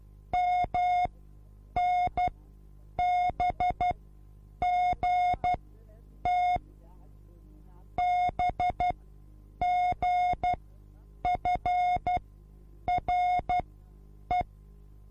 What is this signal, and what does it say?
Morse code, heard as a keyed tone, about 12 wpm, spaced as for 8 wpm, 720 Hz. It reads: MNBGT BGFRE
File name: repos / Morsekurs / DJ4UF Morsen24